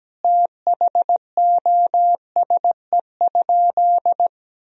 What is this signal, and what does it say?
Morse code, heard as a keyed tone, about 17 wpm, 700 Hz. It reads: THOSE?
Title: Sentences from Top 1000 Words - Review Part 1 - 17wpm